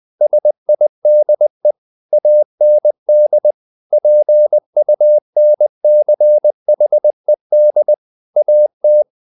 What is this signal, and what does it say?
Morse code, heard as a keyed tone, about 20 wpm, 600 Hz. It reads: SIDE AND PUNCHED AT
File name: the old man and the sea part 2 20wpm